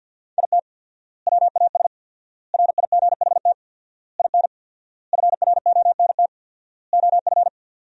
Morse code, heard as a keyed tone, about 53 wpm, 700 Hz. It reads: IT WAS RIGHT IN FRONT OF
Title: 4x Word-Spacing - Sentences from Top 500 Words - 50wpm